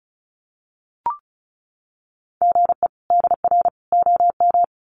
Morse code, heard as a keyed tone, about 34 wpm, 700 Hz. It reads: GE DR OM
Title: Common QSO Phrases 35wpm